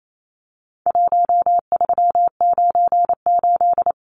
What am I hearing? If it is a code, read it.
Morse code, heard as a keyed tone, about 28 wpm, 700 Hz. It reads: 1398
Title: Numbers - Contest Style - 4-Digit 28wpm